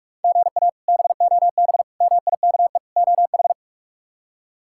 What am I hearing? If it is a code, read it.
Morse code, heard as a keyed tone, about 43 wpm, 700 Hz. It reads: GA BOB MIKE OH
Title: K1USN SST QSO Practice - 45wpm